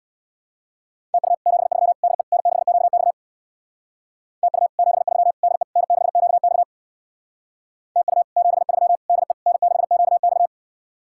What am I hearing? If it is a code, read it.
Morse code, heard as a keyed tone, about 66 wpm, 700 Hz. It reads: TU 73 DE N6QX TU 73 DE N6QX TU 73 DE N6QX